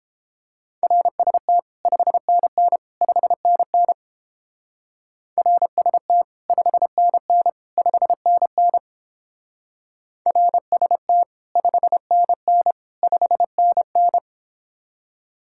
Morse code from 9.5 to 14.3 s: RST 5NN 5NN

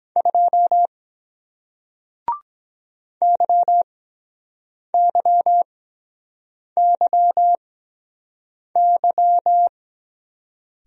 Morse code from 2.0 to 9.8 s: YYYY

